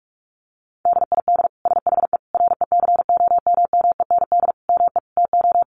Morse code, heard as a keyed tone, about 44 wpm, 700 Hz. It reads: DID SHE RECOMMEND ME TO
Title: Sentences from Top 1200 Words 45wpm